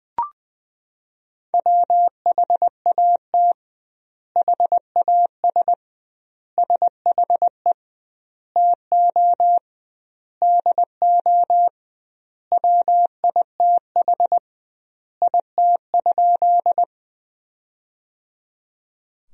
Morse code, heard as a keyed tone, about 20 wpm, 700 Hz. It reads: WHAT HAS SHE TO DO WITH IT?
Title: 2x Word-Spacing - Sentences from Top 100 Words - 20wpm